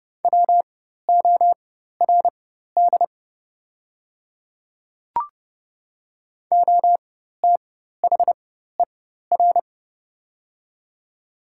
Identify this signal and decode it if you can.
Morse code, heard as a keyed tone, about 30 wpm, spaced as for 14 wpm, 690 Hz. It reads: WORD OTHER
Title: 4x Character Spacing - Top 100 Words - 30wpm